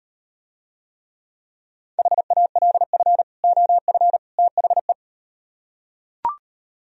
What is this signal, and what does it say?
Morse code, heard as a keyed tone, about 38 wpm, 700 Hz. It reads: HALF OF THE